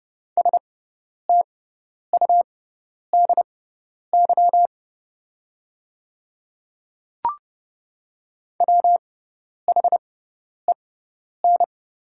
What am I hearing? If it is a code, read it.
Morse code, heard as a keyed tone, about 30 wpm, spaced as for 10 wpm, 700 Hz. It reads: STUDY WHEN
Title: 6x Character Spacing - Top 300 Words - 30wpm